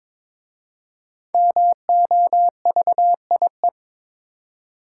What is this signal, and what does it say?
Morse code, heard as a keyed tone, about 22 wpm, 700 Hz. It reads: MOVIE
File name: 5 Letter Words - First 3 Letters Given - 22wpm